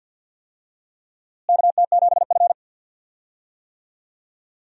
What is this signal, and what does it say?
Morse code, heard as a keyed tone, about 51 wpm, 700 Hz. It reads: KT7F
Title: USA Call Signs - 4 Characters - Repeated 2x - 50wpm